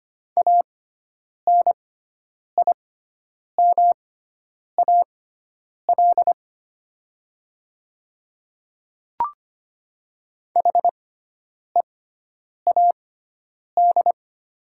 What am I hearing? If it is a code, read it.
Morse code, heard as a keyed tone, about 25 wpm, spaced as for 9 wpm, 700 Hz. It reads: ANIMAL HEAD
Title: 6x Character Spacing - Top 300 Words - 25wpm